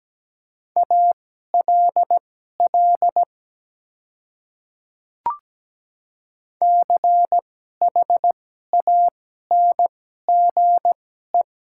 Morse code, heard as a keyed tone, about 17 wpm, spaced as for 12 wpm, 700 Hz. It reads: ALL CHANGE
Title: 2x Character Spacing - Top 200 Words - 17wpm